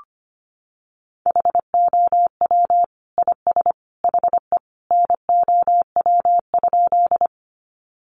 Morse code, heard as a keyed tone, about 25 wpm, 700 Hz. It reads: HOW IS HE NOW?
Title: Sentences from Top 100 Words 25wpm